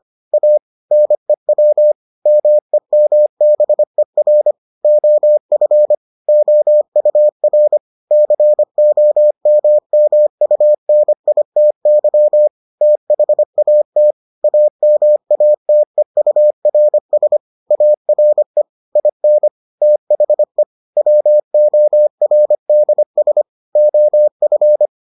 Morse code, heard as a keyed tone, about 25 wpm, 600 Hz. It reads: A NEW MEMBER OF OUR COMMUNITY THAT AMATEURS ARE IN THE WORDS OF